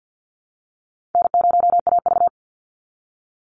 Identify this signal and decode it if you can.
Morse code, heard as a keyed tone, about 50 wpm, 700 Hz. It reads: N0AV
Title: Easy Call Signs Repeated 2x - 50wpm